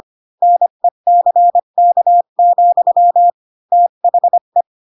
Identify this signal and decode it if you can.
Morse code, heard as a keyed tone, about 25 wpm, 700 Hz. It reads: NECK, THE
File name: Pinocchio / Pinocchio ch3 20 25 ik1hge